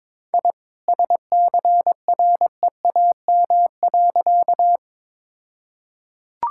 Morse code, heard as a keyed tone, about 22 wpm, 700 Hz. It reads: I SCREAM.
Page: morse-code-ninja-practice-22wpm